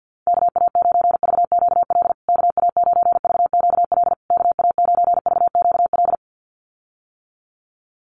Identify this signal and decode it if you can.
Morse code, heard as a keyed tone, about 48 wpm, 700 Hz. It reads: KA9VQL KA9VQL KA9VQL
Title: USA Call Signs Repeated 3x 50wpm